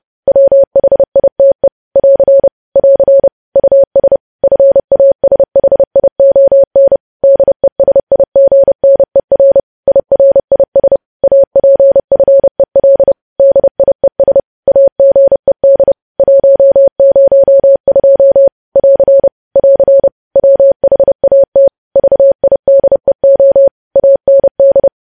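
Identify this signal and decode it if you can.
Morse code, heard as a keyed tone, about 30 wpm, 550 Hz. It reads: WHITE <AR> <AR> US FASHION DESIGNER IRIS APFEL DIES AGED 102 <AR> <AR> WHAT VIDEO AND
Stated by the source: Morse code practice of news headlines at 30 wpm, preceded by vvv.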